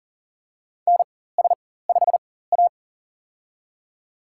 Morse code, heard as a keyed tone, about 40 wpm, spaced as for 19 wpm, 700 Hz. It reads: NS5A
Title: USA Call Signs - 4x Character Spacing - 40wpm